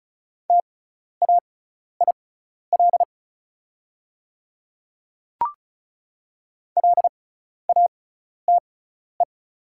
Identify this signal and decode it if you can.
Morse code, heard as a keyed tone, about 34 wpm, spaced as for 12 wpm, 700 Hz. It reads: TAIL LATE
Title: 6x Character Spacing - Top 500 Words - 35wpm